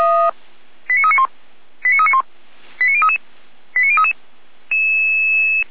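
Five-tone selective call (ZVEI), heard as quick runs of five tones, tones of about 70 ms. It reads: 89391 89391 89033 89033
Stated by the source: ZVEI five-tone call received on an SDR.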